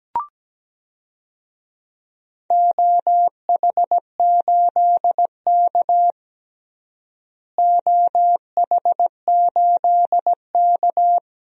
Morse code, heard as a keyed tone, about 17 wpm, 700 Hz. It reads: OH8K OH8K
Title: DXCC-100 Call Signs - 4 Characters - Repeated 3x - 17wpm